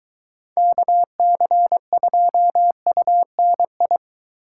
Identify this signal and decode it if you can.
Morse code, heard as a keyed tone, about 23 wpm, 700 Hz. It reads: KC2UNI